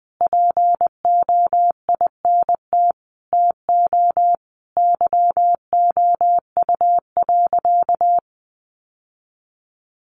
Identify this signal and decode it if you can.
Morse code, heard as a keyed tone, about 20 wpm, 700 Hz. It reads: POINT TO YOU.
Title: Sentences from Top 1100 Words 20wpm